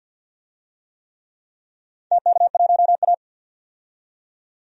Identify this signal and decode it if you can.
Morse code, heard as a keyed tone, about 51 wpm, 700 Hz. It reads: TK1A
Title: DXCC-100 Call Signs - 4 Characters - 50wpm